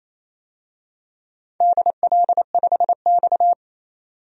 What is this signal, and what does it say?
Morse code, heard as a keyed tone, about 27 wpm, 700 Hz. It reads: DL5X
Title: DXCC-100 Call Signs - 4 Characters - 28wpm